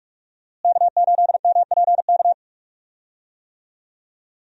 Morse code, heard as a keyed tone, about 42 wpm, 700 Hz. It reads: K8MPK
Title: USA Call Signs - 5 Characters - Repeated 2x - 45wpm